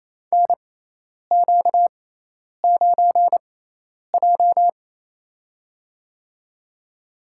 Morse code, heard as a keyed tone, about 28 wpm, spaced as for 10 wpm, 700 Hz. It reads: NQ9J
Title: USA Call Signs - 6x Character Spacing - 28wpm